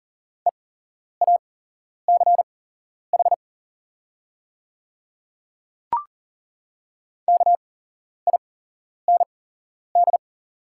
Morse code, heard as a keyed tone, about 40 wpm, spaced as for 11 wpm, 690 Hz. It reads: EACH KIND
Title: 8x Character Spacing - Top 300 Words - 40wpm